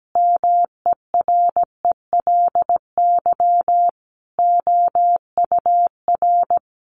Morse code, heard as a keyed tone, about 17 wpm, 700 Hz. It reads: MERELY OUR
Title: Sentences from Top 1400 Words 17wpm